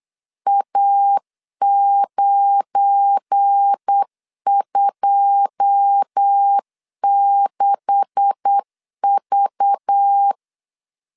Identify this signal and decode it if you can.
Morse code, heard as a keyed tone, about 8 wpm, 800 Hz. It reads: A926V
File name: LEZIONI / 10 SIGNS / 10B INT 8wpm 3534